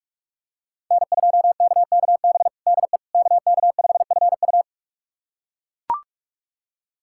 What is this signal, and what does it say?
Morse code, heard as a keyed tone, about 41 wpm, 700 Hz. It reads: N2KKB DE KK5FU
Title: USA Callsign DE Callsign 45wpm